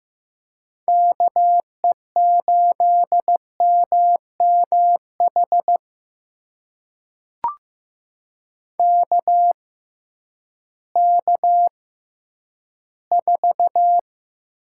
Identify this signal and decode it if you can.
Morse code, heard as a keyed tone, about 15 wpm, spaced as for 10 wpm, 700 Hz. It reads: KE8MMH KK4
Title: USA Call Signs - 6 Characters - 6x Character Spacing - 15wpm